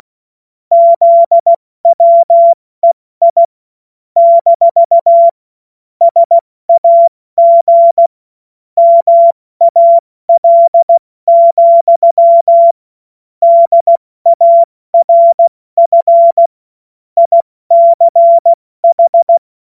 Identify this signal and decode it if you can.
Morse code, heard as a keyed tone, about 16 wpm, 680 Hz. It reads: ZWEI - SAG MAL, DARF ICH
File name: Morsekurs / Witze / Witz 22. August 2023 70 F80